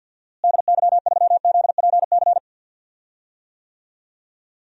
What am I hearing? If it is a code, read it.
Morse code, heard as a keyed tone, about 52 wpm, 700 Hz. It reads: DY3ZPC
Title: DXCC-100 Call Signs - 6 Characters - Repeated 3x - 50wpm